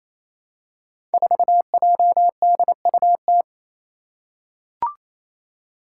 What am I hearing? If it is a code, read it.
Morse code, heard as a keyed tone, about 28 wpm, 700 Hz. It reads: 4JDUT
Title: Five Challenging Letter-Number Combinations 28wpm